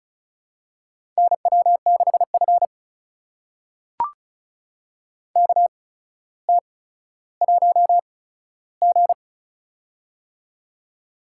Morse code, heard as a keyed tone, about 36 wpm, spaced as for 15 wpm, 700 Hz. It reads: NW6F KT1G
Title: USA Call Signs - 4 Characters - 8x Character Spacing - 35wpm